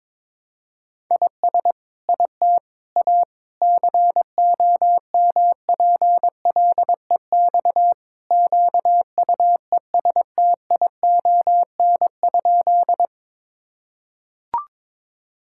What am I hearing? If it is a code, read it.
Morse code, heard as a keyed tone, about 22 wpm, 700 Hz. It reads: IS IT A COMPLEX QUESTION?
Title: Sentences from Top 1000 Words - Review Part 6 - 22wpm